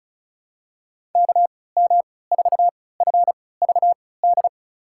Morse code, heard as a keyed tone, about 36 wpm, spaced as for 20 wpm, 700 Hz. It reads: KM4FVD
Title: USA Call Signs - 6 Characters - 3x Character Spacing - 35wpm